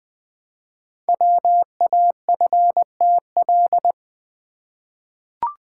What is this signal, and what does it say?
Morse code, heard as a keyed tone, about 20 wpm, 700 Hz. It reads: WAFTL
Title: Code Groups - 5 Letters - 20wpm